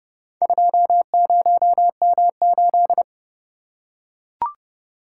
Morse code, heard as a keyed tone, about 28 wpm, 700 Hz. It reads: 20M8